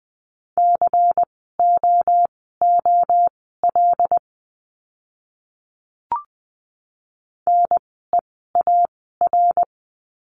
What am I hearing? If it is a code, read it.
Morse code, heard as a keyed tone, about 20 wpm, spaced as for 15 wpm, 700 Hz. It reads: COOL NEAR
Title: 094 Lesson 16f - Letter c - Words 20wpm